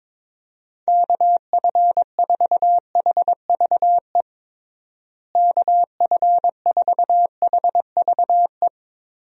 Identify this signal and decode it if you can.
Morse code, heard as a keyed tone, about 22 wpm, 700 Hz. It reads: KF4HVE KF4HVE
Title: USA Call Signs - 6 Characters - Repeated 2x - 22wpm